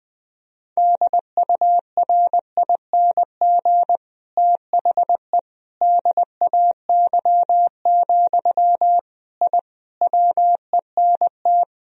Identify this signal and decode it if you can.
Morse code, heard as a keyed tone, about 20 wpm, 700 Hz. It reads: DURING THE DAY, I WENT
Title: Sentences from Top 1000 Words - Review Part 2 - 20wpm